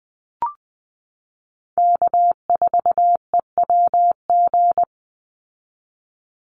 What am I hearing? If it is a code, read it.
Morse code, heard as a keyed tone, about 20 wpm, 700 Hz. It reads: K4EWG
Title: Call Signs Repeated 2x - 20wpm